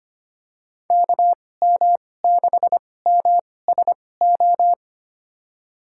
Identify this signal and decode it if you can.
Morse code, heard as a keyed tone, about 25 wpm, spaced as for 18 wpm, 700 Hz. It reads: KM6MSO
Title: USA Call Signs - 6 Characters - 2x Character Spacing - 25wpm